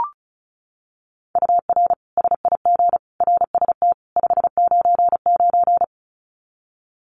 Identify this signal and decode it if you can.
Morse code, heard as a keyed tone, about 34 wpm, 700 Hz. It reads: UR SIG RST 599
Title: Common QSO Phrases 35wpm